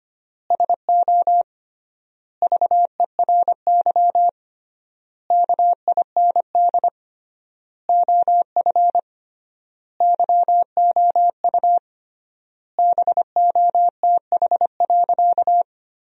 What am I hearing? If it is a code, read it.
Morse code, heard as a keyed tone, about 25 wpm, 700 Hz. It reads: SO VERY KIND OF YOU BOTH.
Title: 3x Word-Spacing - Sentences from Top 300 Words - 25wpm